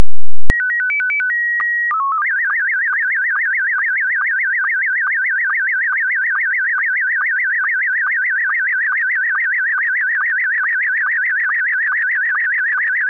SSTV (slow-scan television), Scottie 1.
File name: scottie1